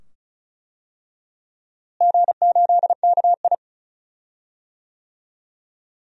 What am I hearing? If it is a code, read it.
Morse code, heard as a keyed tone, about 36 wpm, 700 Hz. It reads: G8KI
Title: DXCC-100 Call Signs - 4 Characters - Repeated 2x - 35wpm